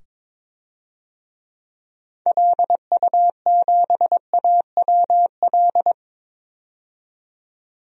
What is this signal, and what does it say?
Morse code, heard as a keyed tone, about 22 wpm, 700 Hz. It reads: LU7AWL
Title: DXCC-100 Call Signs - 6 Characters - 22wpm